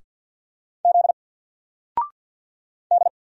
Morse code, heard as a keyed tone, about 50 wpm, 700 Hz. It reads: Z D
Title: Single Letter-Number Alt 50wpm